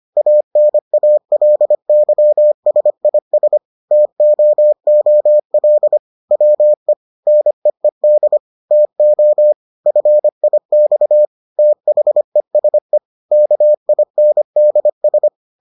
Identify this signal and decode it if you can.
Morse code, heard as a keyed tone, about 25 wpm, 600 Hz. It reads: ANALYSIS TOOL WE NEED TO FIX THESE KINDS